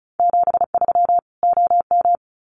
Morse code, heard as a keyed tone, about 34 wpm, 700 Hz. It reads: 73 OM
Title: Common QSO Phrases 35wpm